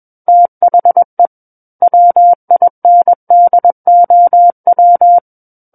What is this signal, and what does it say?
Morse code, heard as a keyed tone, about 21 wpm, 700 Hz. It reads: THE WINDOW
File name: Book One / 12-Chapter 12 - 21 WPM